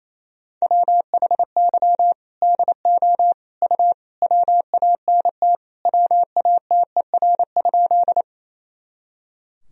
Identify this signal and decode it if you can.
Morse code, heard as a keyed tone, about 28 wpm, 700 Hz. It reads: WHY DO U WANT WATER?
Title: Sentences from Top 200 Words and Abbreviations 28wpm